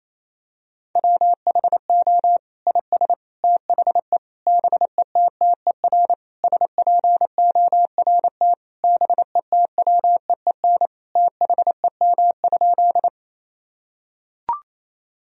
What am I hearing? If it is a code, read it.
Morse code, heard as a keyed tone, about 28 wpm, 700 Hz. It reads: WHO IS THE BETTER SPORT BETWEEN THEM?
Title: Sentences from Top 1000 Words - Review Part 4 - 28wpm